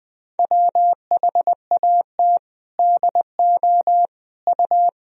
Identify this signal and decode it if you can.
Morse code, heard as a keyed tone, about 20 wpm, 700 Hz. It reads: WHAT DO U
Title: Sentences from Top 200 Words and Abbreviations 20wpm